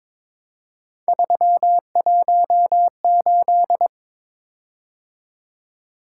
Morse code, heard as a keyed tone, about 23 wpm, 700 Hz. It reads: 318
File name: practice / 22wpm / Three Letter-Number Combinations - 22wpm